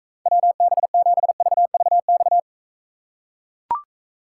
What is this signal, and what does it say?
Morse code, heard as a keyed tone, about 45 wpm, 700 Hz. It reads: WB7VVX